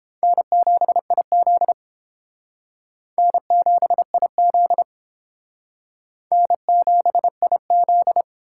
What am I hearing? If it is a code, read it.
Morse code, heard as a keyed tone, about 29 wpm, 700 Hz. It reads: N7IZ N7IZ N7IZ